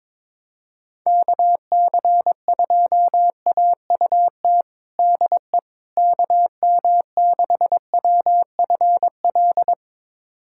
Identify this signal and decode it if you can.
Morse code, heard as a keyed tone, about 21 wpm, 700 Hz. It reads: KC2AUT DE KM6WFL
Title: USA Callsign DE Callsign - 6 Characters - 22wpm